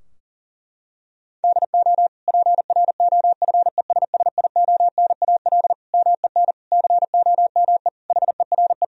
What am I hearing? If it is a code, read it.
Morse code, heard as a keyed tone, about 40 wpm, 700 Hz. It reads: DO PROFESSIONAL MEN COME HERE
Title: Sentences from Top 1000 Words - Review Part 4 - 40wpm